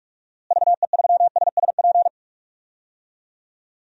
Morse code, heard as a keyed tone, about 46 wpm, 700 Hz. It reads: VE3SSP
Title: DXCC-100 Call Signs - 6 Characters - Repeated 3x - 45wpm